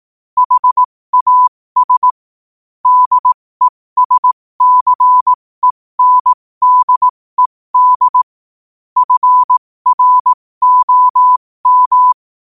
Morse code, heard as a keyed tone, about 18 wpm, spaced as for 15 wpm, 1000 Hz. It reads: HAS DESCENDED FROM
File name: qotdfiles / 15wpm